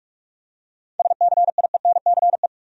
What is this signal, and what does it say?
Morse code, heard as a keyed tone, about 46 wpm, 700 Hz. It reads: SCIENCE